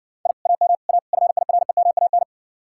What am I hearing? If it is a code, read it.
Morse code, heard as a keyed tone, about 62 wpm, 700 Hz. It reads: I AM A FIREMAN